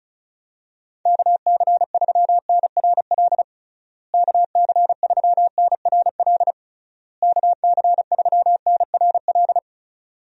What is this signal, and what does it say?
Morse code, heard as a keyed tone, about 36 wpm, 700 Hz. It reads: KC3NRL KC3NRL KC3NRL
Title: USA Call Signs - 6 Characters - Repeated 3x - 35wpm